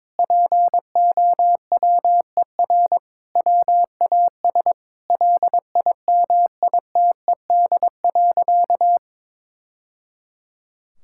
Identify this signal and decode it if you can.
Morse code, heard as a keyed tone, about 21 wpm, 700 Hz. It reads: POWER WAS LIMITED.
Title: Sentences from Top 1000 Words - Review Part 6 - 22wpm